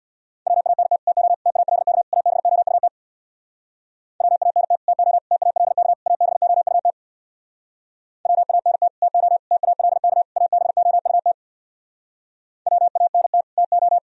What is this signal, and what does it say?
Morse code, heard as a keyed tone, about 64 wpm, 700 Hz. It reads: WANT TO TALK ABOUT WANT TO TALK ABOUT WANT TO TALK ABOUT WANT TO